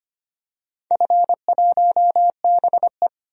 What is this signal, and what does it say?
Morse code, heard as a keyed tone, about 25 wpm, 700 Hz. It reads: F1BE
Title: DXCC-100 Call Signs - 4 Characters - 25wpm